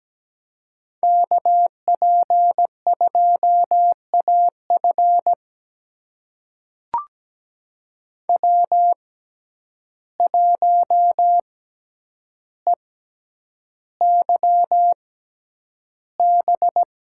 Morse code, heard as a keyed tone, about 17 wpm, spaced as for 9 wpm, 700 Hz. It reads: KP2AF W1EYB
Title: USA Call Signs - 5 Characters - 6x Character Spacing - 17wpm